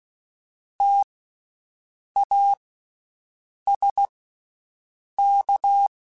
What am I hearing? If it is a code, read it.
Morse code, heard as a keyed tone, about 16 wpm, spaced as for 6 wpm, 800 Hz. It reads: TASK
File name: Special INGs-mp3s / Special INGs-mp3s / Slow ing @15